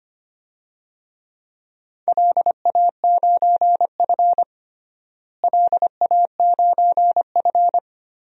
Morse code, heard as a keyed tone, about 25 wpm, 700 Hz. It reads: LA9F LA9F
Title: DXCC-100 Call Signs - 4 Characters - Repeated 2x - 25wpm